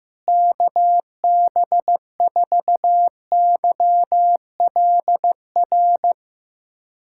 Morse code, heard as a keyed tone, about 15 wpm, 700 Hz. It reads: KB4YLR
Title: USA Callsign DE Callsign - 6 Characters - 15wpm